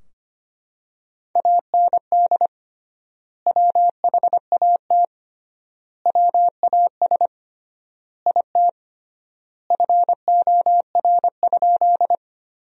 Morse code, heard as a keyed tone, about 25 wpm, 700 Hz. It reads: AND WHAT WAS IT FOR?
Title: 3x Word-Spacing - Sentences from Top 100 Words - 25wpm